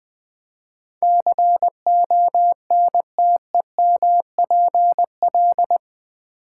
Morse code, heard as a keyed tone, about 20 wpm, 700 Hz. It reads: CONTEMPL
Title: Build A Word - 4 Letters At A Time - 11-Letter Words - 20wpm